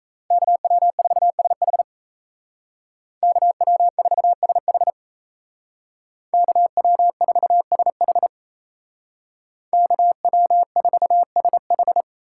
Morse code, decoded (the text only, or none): KW4SH KW4SH KW4SH KW4SH